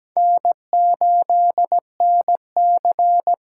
Morse code, heard as a keyed tone, about 17 wpm, 700 Hz. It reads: N8NC
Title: USA Call Signs - 5 Characters - 8x Character Spacing - 17wpm